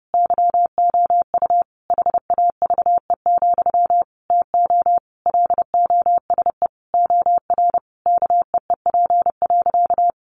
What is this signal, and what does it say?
Morse code, heard as a keyed tone, about 28 wpm, 700 Hz. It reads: YOU HAVE, TO LOSE OR KEEP.
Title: Sentences from Top 1100 Words 30wpm